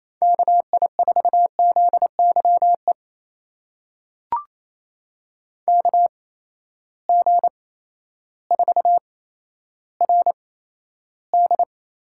Morse code, heard as a keyed tone, about 28 wpm, spaced as for 13 wpm, 700 Hz. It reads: KI4ZYE KG4RD
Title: USA Call Signs - 6 Characters - 8x Character Spacing - 28wpm